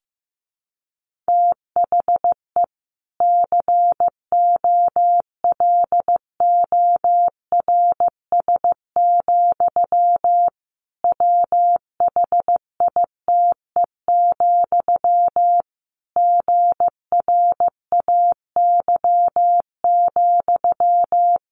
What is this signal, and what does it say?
Morse code, heard as a keyed tone, about 15 wpm, 700 Hz. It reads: THE COLORS, WHITE, GRAY,
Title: A Sound Of Thunder 15wpm